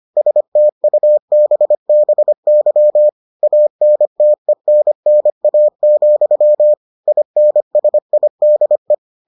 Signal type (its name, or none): Morse code